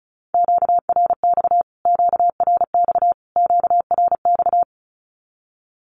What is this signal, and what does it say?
Morse code, heard as a keyed tone, about 34 wpm, 700 Hz. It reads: QRX QRX QRX
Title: Q-Codes Repeated 3x 35wpm